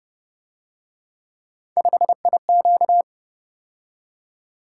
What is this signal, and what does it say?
Morse code, heard as a keyed tone, about 31 wpm, 700 Hz. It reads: 5IQ
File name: Speed-Racing - Three Letter-Number Combinations - 25wpm